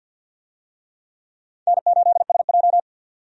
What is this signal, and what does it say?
Morse code, heard as a keyed tone, about 49 wpm, 700 Hz. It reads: N8SJ
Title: USA Call Signs - 4 Characters - 50wpm